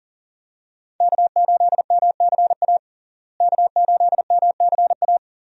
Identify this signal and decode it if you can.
Morse code, heard as a keyed tone, about 40 wpm, 700 Hz. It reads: K8MCA K8MCA
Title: USA Call Signs - 5 Characters - Repeated 2x - 40wpm